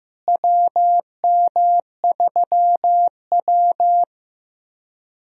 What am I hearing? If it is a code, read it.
Morse code, heard as a keyed tone, about 15 wpm, 700 Hz. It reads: WM3W
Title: USA Call Signs - 4 Characters - Repeated 2x - 15wpm